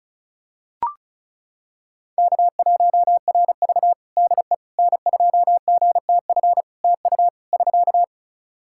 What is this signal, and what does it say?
Morse code, heard as a keyed tone, about 34 wpm, 700 Hz. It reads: K1RV DE N2GTF TU <SK>
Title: Practice QSO - Exchange 4 - Ending - 35wpm